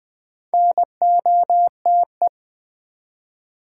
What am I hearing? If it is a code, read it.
Morse code, heard as a keyed tone, about 20 wpm, 700 Hz. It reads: NOTE
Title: Build A Word - 2 Letters At A Time - 8-Letter Words - 20wpm